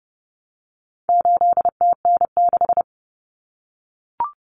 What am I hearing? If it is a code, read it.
Morse code, heard as a keyed tone, about 30 wpm, 700 Hz. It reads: 8TN6
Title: Common Cut Numbers - 4-Digit 30wpm